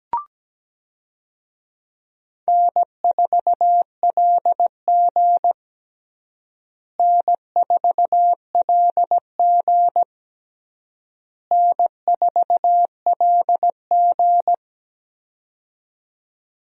Morse code, heard as a keyed tone, about 17 wpm, 700 Hz. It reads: N4LG N4LG N4LG
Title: USA Call Signs - 4 Characters - Repeated 3x - 17wpm